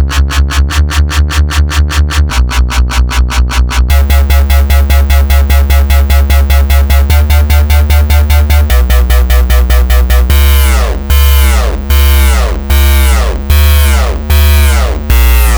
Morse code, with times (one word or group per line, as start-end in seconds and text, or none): none